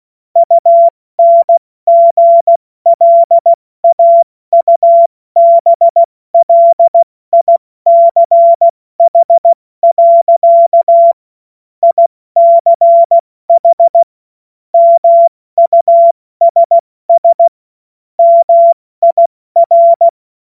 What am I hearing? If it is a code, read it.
Morse code, heard as a keyed tone, about 16 wpm, 680 Hz. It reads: UNGLAUBLICH. ICH MUSS MIR
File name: Morsekurs / Witze / Witz 29. Juni 2021, 70 F80